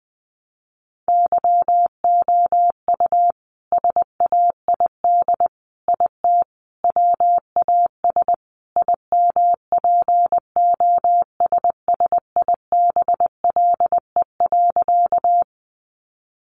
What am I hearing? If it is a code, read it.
Morse code, heard as a keyed tone, about 20 wpm, 700 Hz. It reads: YOU SAID IT WAS IMPOSSIBLE.
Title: Sentences from Top 1100 Words 20wpm